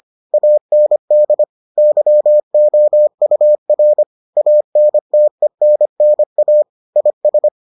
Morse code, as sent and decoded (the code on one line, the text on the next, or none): .- -. -.. / -.-- --- ..- .-. / .- -. - . -. -. .- / .. ...
AND YOUR ANTENNA IS